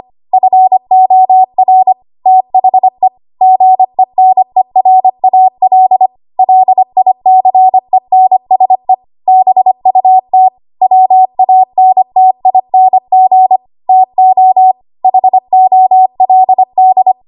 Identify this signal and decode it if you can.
Morse code, heard as a keyed tone, about 25 wpm, 750 Hz. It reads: FOR THE GENERAL LICENSE BUT WANTING TO HOLD
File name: Archive / 25 WPM / 25WPM